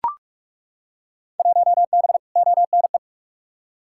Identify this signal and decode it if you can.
Morse code, heard as a keyed tone, about 41 wpm, 700 Hz. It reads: 1B ONE
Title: Field Day Exchanges - 45wpm